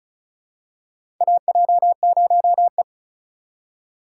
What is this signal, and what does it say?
Morse code, heard as a keyed tone, about 34 wpm, 700 Hz. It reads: AJ0E
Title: USA Call Signs - 4 Characters - 2x Character Spacing - 35wpm